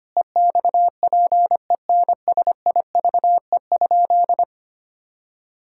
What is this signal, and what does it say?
Morse code, heard as a keyed tone, about 25 wpm, 700 Hz. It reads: EXPENSIVE?
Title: Sentences from Top 1000 Words - Review Part 6 - 25wpm